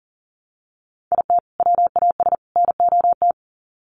Morse code, heard as a keyed tone, about 40 wpm, 700 Hz. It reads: IT WAS NOT
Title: Sets of 3 Words 40wpm